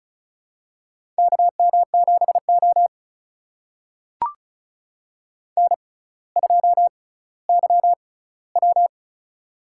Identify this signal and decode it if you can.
Morse code, heard as a keyed tone, about 36 wpm, spaced as for 18 wpm, 700 Hz. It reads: KM7O N2YW
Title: USA Call Signs - 4 Characters - 6x Character Spacing - 35wpm